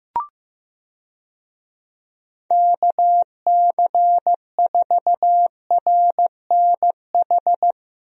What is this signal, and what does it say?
Morse code, heard as a keyed tone, about 15 wpm, 700 Hz. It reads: KC4RNH
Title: USA Call Signs - 6 Characters - Repeated 3x - 15wpm